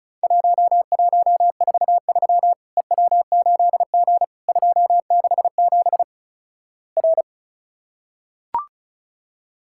Morse code, heard as a keyed tone, about 36 wpm, 700 Hz. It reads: 1143 EW8G 267 R
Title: Worked All Europe DX Contest - QTC Exchange - 35wpm